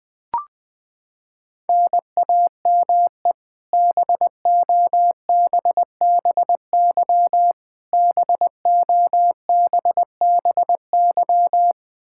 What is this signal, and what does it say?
Morse code, heard as a keyed tone, about 21 wpm, 700 Hz. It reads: NAME BOBBY BOBBY
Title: Practice QSO - Exchange 1 - Beginning - 20wpm